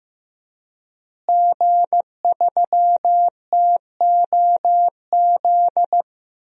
Morse code, heard as a keyed tone, about 15 wpm, 700 Hz. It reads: G3TOZ
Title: DXCC-100 Call Signs - 5 Characters - Repeated 3x - 15wpm